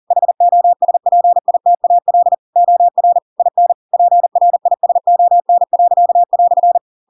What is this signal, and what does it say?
Morse code, heard as a keyed tone, about 40 wpm, 700 Hz. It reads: HOSPITAL OR IN PRISON."
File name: Pinocchio / Pinocchio ch4 40 40 ik1hge